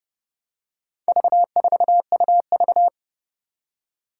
Morse code, heard as a keyed tone, about 30 wpm, 700 Hz. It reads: V4UV
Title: Four Challenging Letter-Number Combinations 30wpm